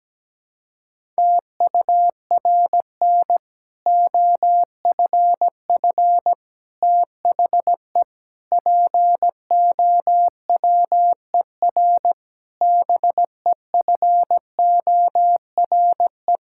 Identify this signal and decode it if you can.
Morse code, heard as a keyed tone, about 17 wpm, 700 Hz. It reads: TURN OFF THE POWER BEFORE
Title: Sentences from Top 1000 Words - Review Part 6 - 17wpm